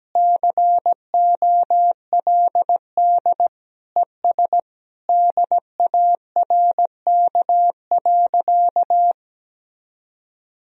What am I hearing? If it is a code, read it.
Morse code, heard as a keyed tone, about 17 wpm, 700 Hz. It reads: COLD ES DARK.
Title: Sentences from Top 500 Words and Abbreviations 17wpm